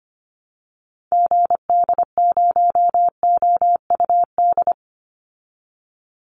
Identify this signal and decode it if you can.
Morse code, heard as a keyed tone, about 25 wpm, 700 Hz. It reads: GD0OUD
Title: Call Signs Repeated 3x - 25wpm